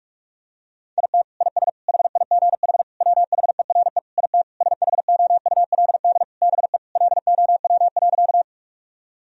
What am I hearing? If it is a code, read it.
Morse code, heard as a keyed tone, about 46 wpm, 700 Hz. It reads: IT IS HIGH WHERE IT SHOULD BE LOW.
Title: Sentences from Top 1000 Words - Review Part 1 - 45wpm